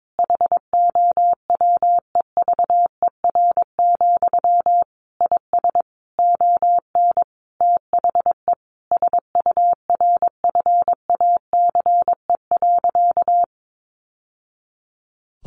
Morse code, heard as a keyed tone, about 22 wpm, 700 Hz. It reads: HOWEVER, IS ON THE SURFACE.
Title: Sentences from Top 600 Words 22wpm